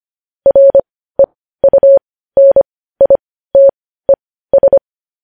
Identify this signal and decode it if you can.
Morse code, heard as a keyed tone, about 25 wpm, spaced as for 15 wpm, 550 Hz. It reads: REUNITES
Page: news-headlines-in-morse-code-at-15-wpm